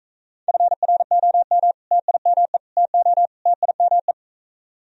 Morse code, heard as a keyed tone, about 41 wpm, 700 Hz. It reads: FROM TIME TO TIME